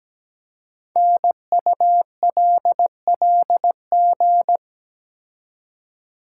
Morse code, heard as a keyed tone, about 17 wpm, 700 Hz. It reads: NULLG
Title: Code Groups - 5 Letter:Numbers - 17wpm